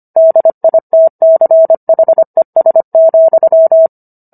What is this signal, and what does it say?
Morse code, heard as a keyed tone, about 25 wpm, 650 Hz. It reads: DITCHES,